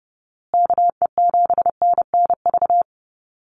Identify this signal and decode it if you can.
Morse code, heard as a keyed tone, about 30 wpm, 700 Hz. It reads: KE7NNV
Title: USA Call Signs Repeated 2x 30wpm